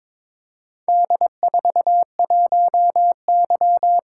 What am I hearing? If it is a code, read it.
Morse code, heard as a keyed tone, about 23 wpm, 700 Hz. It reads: D41Y
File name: practice / 22wpm / Four Letter-Number Combinations - 22wpm